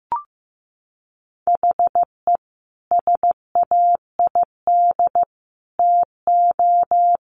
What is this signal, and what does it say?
Morse code, heard as a keyed tone, about 15 wpm, 700 Hz. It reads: HE SAID TO
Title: Sentences from Top 1100 Words 15wpm